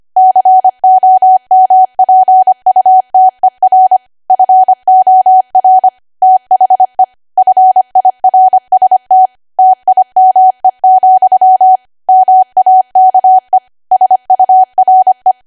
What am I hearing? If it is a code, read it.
Morse code, heard as a keyed tone, about 25 wpm, 750 Hz. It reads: COMPUTER FOR THE FIRST TIME, MAKE SURE
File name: cwa / W1AW-Level3 / 25WPM